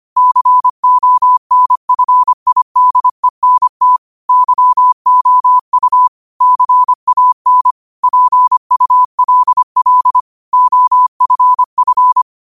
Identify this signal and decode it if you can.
Morse code, heard as a keyed tone, about 25 wpm, 1000 Hz. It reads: CONFIDENT YOU CAN PULL OFF